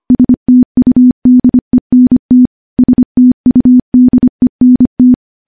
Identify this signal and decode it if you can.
Morse code, heard as a keyed tone, about 25 wpm, 250 Hz. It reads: STUDENT STUDENT
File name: morse / morse-training-25WPM